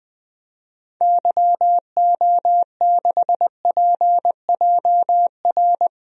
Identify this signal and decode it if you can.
Morse code, heard as a keyed tone, about 20 wpm, 700 Hz. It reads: YO6PJR
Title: DXCC-100 Call Signs - 6 Characters - 20wpm